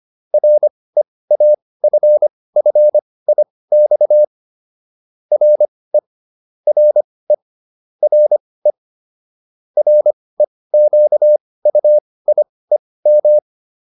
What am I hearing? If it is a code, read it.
Morse code, heard as a keyed tone, about 25 wpm, spaced as for 18 wpm, 600 Hz. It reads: REAFFIX RE RE RE REQUIEM